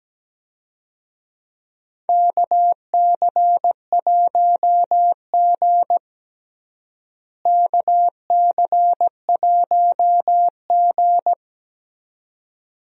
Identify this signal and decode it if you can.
Morse code, heard as a keyed tone, about 17 wpm, 700 Hz. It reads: KC1G KC1G
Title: USA Call Signs - 4 Characters - Repeated 2x - 17wpm